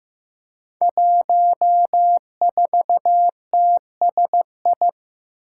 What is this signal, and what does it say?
Morse code, heard as a keyed tone, about 15 wpm, 700 Hz. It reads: 14TSI
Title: Five Challenging Letter-Number Combinations 15wpm